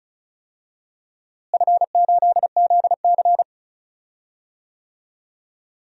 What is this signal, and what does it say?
Morse code, heard as a keyed tone, about 36 wpm, 700 Hz. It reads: F8ZC
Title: DXCC-100 Call Signs - 4 Characters - Repeated 3x - 35wpm